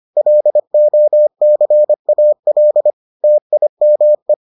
Morse code, heard as a keyed tone, about 25 wpm, 600 Hz. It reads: LOCAL TIME